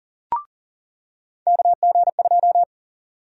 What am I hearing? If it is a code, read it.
Morse code, heard as a keyed tone, about 39 wpm, 700 Hz. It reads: KG2
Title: USA Call Sign Prefixes - 40wpm